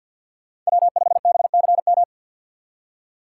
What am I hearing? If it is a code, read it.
Morse code, heard as a keyed tone, about 58 wpm, 700 Hz. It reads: W5BCM
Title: Speed Racing - USA Call Signs 50wpm